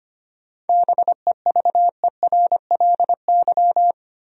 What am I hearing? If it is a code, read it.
Morse code, heard as a keyed tone, about 25 wpm, 700 Hz. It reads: BEVERLY
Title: USA Common Names - 25wpm